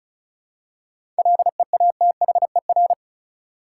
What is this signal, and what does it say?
Morse code, heard as a keyed tone, about 34 wpm, 700 Hz. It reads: LEATHER
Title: Seven Letter Words - 35wpm